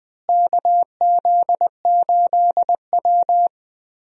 Morse code, heard as a keyed tone, about 20 wpm, 700 Hz. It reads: KZ8W
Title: USA Call Signs - 4 Characters - Repeated 3x - 20wpm